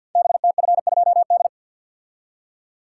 Morse code, heard as a keyed tone, about 52 wpm, 690 Hz. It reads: BTF2D
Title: Code Groups - 5 Letter:Numbers - 50wpm